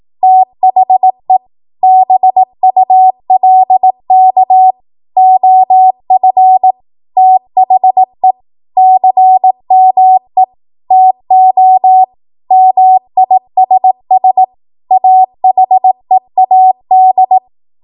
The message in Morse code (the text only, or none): THE BULK OF THE CME TO MISS AHEAD